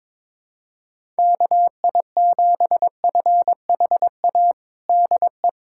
Morse code, heard as a keyed tone, about 22 wpm, 700 Hz. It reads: KI7FHA DE
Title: USA Callsign DE Callsign 22wpm